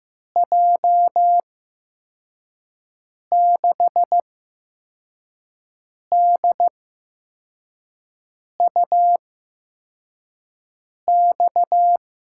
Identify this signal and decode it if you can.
Morse code, heard as a keyed tone, about 15 wpm, spaced as for 4 wpm, 700 Hz. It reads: J6DUX
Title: USA Call Signs - 6 Characters - 8x Character Spacing - 15wpm